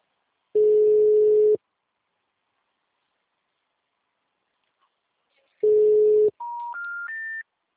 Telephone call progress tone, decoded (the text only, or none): ringback tone